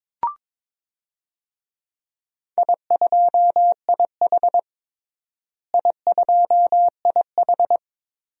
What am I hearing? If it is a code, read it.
Morse code, heard as a keyed tone, about 22 wpm, 700 Hz. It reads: I2IH I2IH
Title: DXCC-100 Call Signs - 4 Characters - Repeated 2x - 22wpm